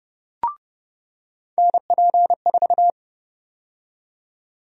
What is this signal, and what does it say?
Morse code, heard as a keyed tone, about 32 wpm, 700 Hz. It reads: NP4
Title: USA Call Sign Prefixes - 30wpm